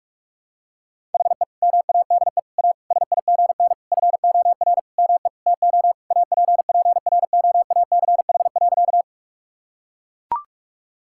Morse code, heard as a keyed tone, about 42 wpm, 700 Hz. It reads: HE MADE A SIGN FOR ME TO APPROACH.
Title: Sentences from Top 1000 Words - Review Part 6 - 45wpm